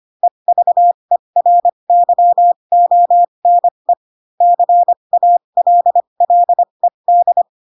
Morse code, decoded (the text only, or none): EVERYONE CALLED